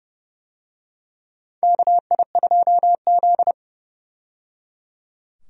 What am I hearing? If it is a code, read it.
Morse code, heard as a keyed tone, about 29 wpm, 700 Hz. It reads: KI2Z
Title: USA Call Signs - 4 Characters - 30wpm